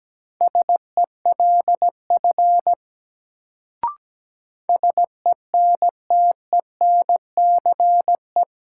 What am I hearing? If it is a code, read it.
Morse code, heard as a keyed tone, about 17 wpm, 700 Hz. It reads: SELF SENTENCE
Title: Sentences from Top 1000 Words - Review Part 1 - New Word Drill - 17wpm